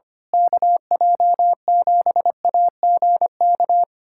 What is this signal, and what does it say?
Morse code, heard as a keyed tone, about 25 wpm, 700 Hz. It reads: KJ7AGK
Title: USA Callsign DE Callsign 25wpm